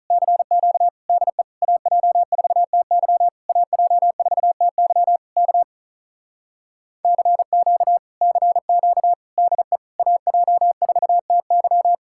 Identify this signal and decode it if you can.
Morse code, heard as a keyed tone, about 48 wpm, 700 Hz. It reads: CQ DE AJ4TY AJ4TY K CQ CQ DE AJ4TY